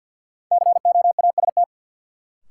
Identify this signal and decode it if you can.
Morse code, heard as a keyed tone, about 52 wpm, 700 Hz. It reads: COAST
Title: Sentences from Top 1000 Words - Review Part 4 - New Word Drill - 50wpm